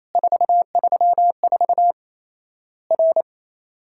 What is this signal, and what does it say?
Morse code, heard as a keyed tone, about 28 wpm, 700 Hz. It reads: 434 R
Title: Worked All Europe DX Contest - QTC Exchange - 28wpm